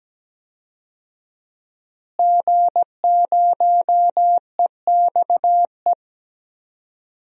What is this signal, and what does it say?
Morse code, heard as a keyed tone, about 17 wpm, 700 Hz. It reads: G0EXE
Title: DXCC-100 Call Signs - 5 Characters - Repeated 2x - 17wpm